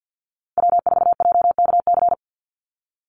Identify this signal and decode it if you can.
Morse code, heard as a keyed tone, about 50 wpm, 700 Hz. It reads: W4JKC
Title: NA SOTA Chaser Callsigns 50wpm